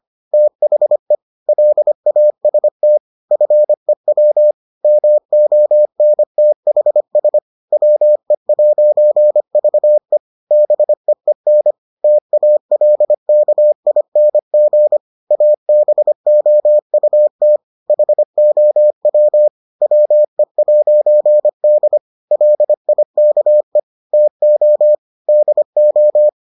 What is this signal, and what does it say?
Morse code, heard as a keyed tone, about 25 wpm, 600 Hz. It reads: THE LAST FEW MONTHS WE'VE BEEN TALKING ABOUT HOW WE'D LIKE TO DO